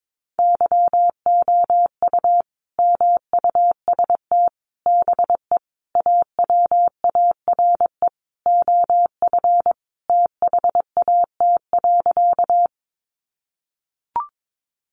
Morse code, heard as a keyed tone, about 22 wpm, 700 Hz. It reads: YOU MUST BE AWARE OF THAT.
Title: Sentences from Top 1200 Words 22wpm